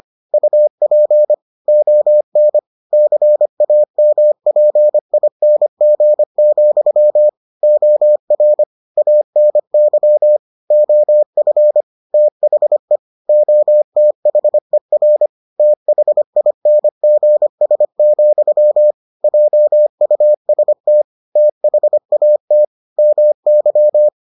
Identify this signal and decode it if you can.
Morse code, heard as a keyed tone, about 25 wpm, 600 Hz. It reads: UP ON CAMPING, OR ANY OF THE OTHER THINGS, JUST THAT MY